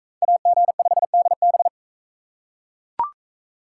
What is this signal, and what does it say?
Morse code, heard as a keyed tone, about 41 wpm, 700 Hz. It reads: AG5DB